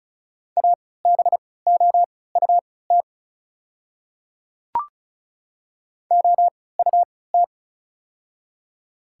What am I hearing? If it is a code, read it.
Morse code, heard as a keyed tone, about 34 wpm, spaced as for 20 wpm, 690 Hz. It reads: ABOUT OUT
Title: 3x Character Spacing - Top 100 Words - 35wpm